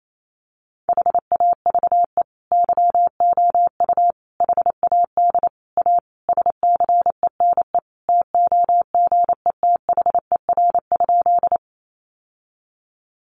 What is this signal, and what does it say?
Morse code, heard as a keyed tone, about 28 wpm, 700 Hz. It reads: HAVE YOU HAD A SCENE TOGETHER?
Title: Sentences from Top 1100 Words 28wpm